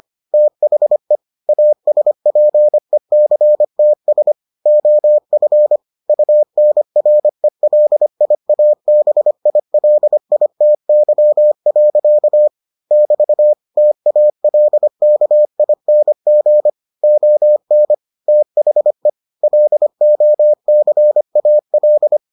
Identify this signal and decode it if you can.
Morse code, heard as a keyed tone, about 25 wpm, 600 Hz. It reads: THE ASPECTS OF UNRELIABILITY. <BT> TALKING ON THE LOCAL